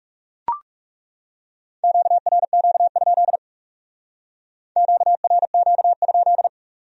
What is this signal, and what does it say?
Morse code, heard as a keyed tone, about 37 wpm, 700 Hz. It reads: QRQ? QRQ?
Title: Speed-Racing Q-Codes - 30wpm